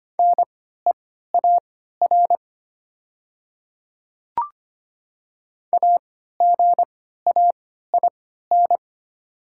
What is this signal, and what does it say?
Morse code, heard as a keyed tone, about 25 wpm, spaced as for 14 wpm, 700 Hz. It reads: NEAR AGAIN
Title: 3x Character Spacing - Top 300 Words - 25wpm